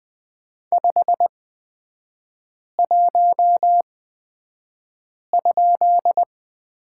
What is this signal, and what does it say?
Morse code, heard as a keyed tone, about 20 wpm, spaced as for 5 wpm, 700 Hz. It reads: 51?
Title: 255 Lesson 41e - Period - Three Characters 20wpm